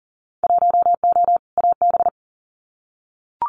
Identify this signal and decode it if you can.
Morse code, heard as a keyed tone, about 40 wpm, 700 Hz. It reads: 1O AB
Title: Winter Field Day Exchanges Encore 40wpm